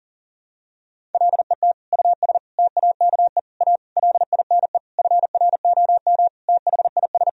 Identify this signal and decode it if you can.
Morse code, heard as a keyed tone, about 40 wpm, 700 Hz. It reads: LET US TAKE A LINE FROM THIS
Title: Sentences from Top 1000 Words - Review Part 1 - 40wpm